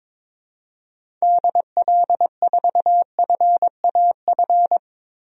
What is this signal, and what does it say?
Morse code, heard as a keyed tone, about 21 wpm, 700 Hz. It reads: DL4FAF
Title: DXCC-100 Call Signs - 6 Characters - Repeated 3x - 22wpm